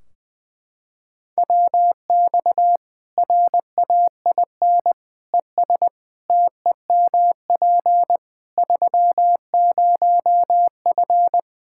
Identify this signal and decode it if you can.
Morse code, heard as a keyed tone, about 20 wpm, 700 Hz. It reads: WX RAIN ES TEMP 30F
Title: K1USN SST QSO Practice 20wpm